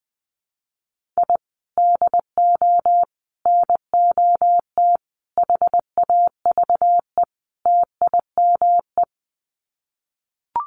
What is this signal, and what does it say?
Morse code, heard as a keyed tone, about 20 wpm, 700 Hz. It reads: I DO NOT HAVE TIME
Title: Sets of 5 Words Encore 20wpm